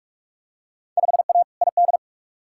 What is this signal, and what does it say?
Morse code, heard as a keyed tone, about 47 wpm, 700 Hz. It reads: 5A ID
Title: Field Day Exchanges Encore - 45wpm